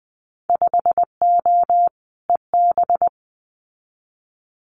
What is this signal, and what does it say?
Morse code, heard as a keyed tone, about 20 wpm, 700 Hz. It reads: 5O EB
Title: Winter Field Day Exchanges Encore 20wpm